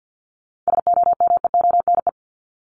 Morse code, heard as a keyed tone, about 50 wpm, 700 Hz. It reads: SOMEONE
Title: Top 501-750 Words 50wpm